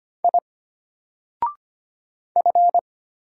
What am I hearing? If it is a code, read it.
Morse code, heard as a keyed tone, about 25 wpm, 700 Hz. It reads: I F
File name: Single-Letter-Number-25wpm